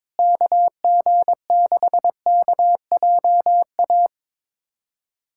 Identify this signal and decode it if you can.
Morse code, heard as a keyed tone, about 22 wpm, 700 Hz. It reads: KG6KJA
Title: USA Callsign DE Callsign - 6 Characters - 22wpm